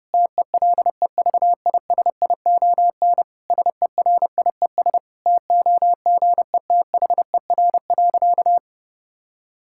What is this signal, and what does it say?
Morse code, heard as a keyed tone, about 30 wpm, 700 Hz. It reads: TELEVISION SERIES TOGETHER.
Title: Sentences from Top 1000 Words - Review Part 5 - 30wpm